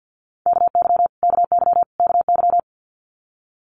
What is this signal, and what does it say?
Morse code, heard as a keyed tone, about 49 wpm, 700 Hz. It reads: KY KY KY
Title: US States Repeated 3x 50wpm